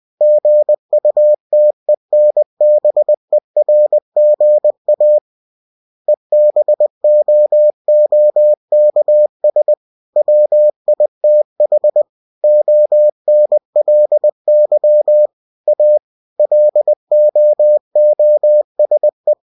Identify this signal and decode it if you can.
Morse code, heard as a keyed tone, about 20 wpm, 600 Hz. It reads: GUTENBERGA EBOOKS WITH ONLY A LOOSE
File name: cw-books / winnie-the-pooh / winnie-20wpm-page-0122